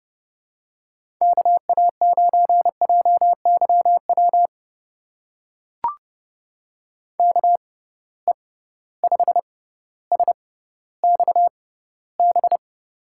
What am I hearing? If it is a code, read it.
Morse code, heard as a keyed tone, about 30 wpm, spaced as for 15 wpm, 700 Hz. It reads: KA9JYW KE5SXB
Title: USA Call Signs - 6 Characters - 6x Character Spacing - 30wpm